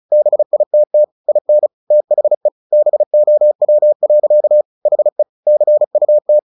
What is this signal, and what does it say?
Morse code, heard as a keyed tone, about 34 wpm, 600 Hz. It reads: BITT IN THE BOW. HE CUT
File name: the old man and the sea 2 35wpm